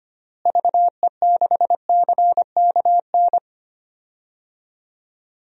Morse code, heard as a keyed tone, about 25 wpm, 700 Hz. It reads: VE6CKN
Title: DXCC-100 Call Signs - 6 Characters - Repeated 3x - 25wpm